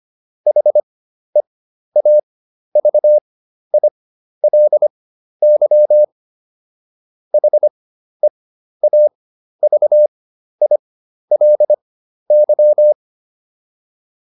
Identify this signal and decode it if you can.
Morse code, heard as a keyed tone, about 25 wpm, spaced as for 12 wpm, 600 Hz. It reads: HEAVILY HEAVILY